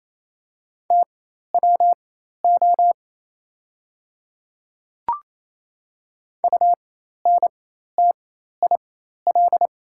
Morse code, heard as a keyed tone, about 28 wpm, spaced as for 13 wpm, 700 Hz. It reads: TWO UNTIL
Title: 4x Character Spacing - Top 300 Words - 28wpm